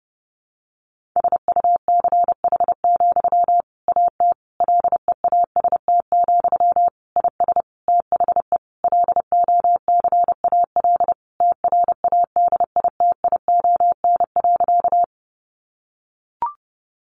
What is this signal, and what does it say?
Morse code, heard as a keyed tone, about 30 wpm, 700 Hz. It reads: SUCH, AT LEAST, IS THE LOCAL TRADITION.
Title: Sentences from Top 1500 Words 30wpm